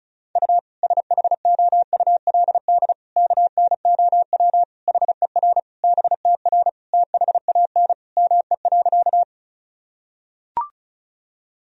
Morse code, heard as a keyed tone, about 36 wpm, 700 Hz. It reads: U SHOULD KNOW HER BTR THAN ME.
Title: Sentences from Top 400 Words and Abbreviations 35wpm